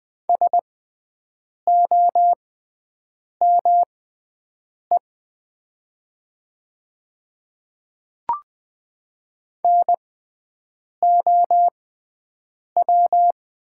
Morse code, heard as a keyed tone, about 20 wpm, spaced as for 7 wpm, 700 Hz. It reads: SOME NOW
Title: 6x Character Spacing - Top 100 Words - 20wpm